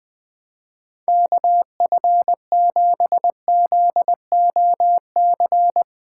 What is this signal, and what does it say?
Morse code, heard as a keyed tone, about 20 wpm, 700 Hz. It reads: KF7ZOC
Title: Speed Racing - USA 2x3 Call Signs 17wpm